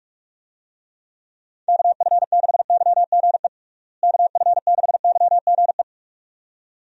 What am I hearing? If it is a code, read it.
Morse code, heard as a keyed tone, about 46 wpm, 700 Hz. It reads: KF6YGE KF6YGE
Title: USA Call Signs - 6 Characters - Repeated 2x - 45wpm